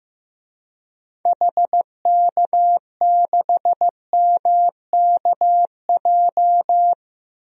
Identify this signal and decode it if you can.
Morse code, heard as a keyed tone, about 15 wpm, 700 Hz. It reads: HK6MKJ
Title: DXCC-100 Call Signs - 6 Characters - Repeated 3x - 15wpm